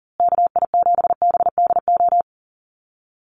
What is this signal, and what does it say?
Morse code, heard as a keyed tone, about 40 wpm, 700 Hz. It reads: KI7BDO
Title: USA Call Signs Repeated 2x 40wpm